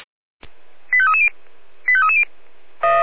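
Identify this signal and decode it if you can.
Five-tone selective call (ZVEI), heard as quick runs of five tones, tones of about 70 ms. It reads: 85330 85330
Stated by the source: ZVEI five-tone selective call 85259 received by SDR.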